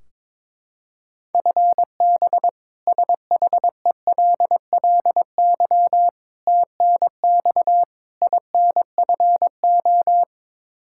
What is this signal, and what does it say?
Morse code, heard as a keyed tone, about 22 wpm, 700 Hz. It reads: FB SHELLY TNX INFO
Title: K1USN SST QSO Practice 22wpm